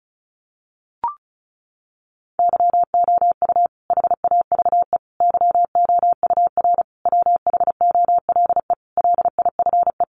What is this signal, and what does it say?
Morse code, heard as a keyed tone, about 34 wpm, 700 Hz. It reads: YOU HAVE YOUR WHOLE LIFE
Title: Sentences from Top 400 Words 35wpm